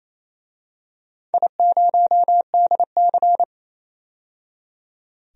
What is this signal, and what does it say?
Morse code, heard as a keyed tone, about 28 wpm, 700 Hz. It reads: I0DC
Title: DXCC-100 Call Signs - 4 Characters - 28wpm